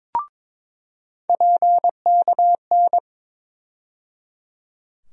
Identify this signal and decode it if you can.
Morse code, heard as a keyed tone, about 22 wpm, 700 Hz. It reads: PKN
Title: USA Call Sign Postfixes - 3 Letters 22wpm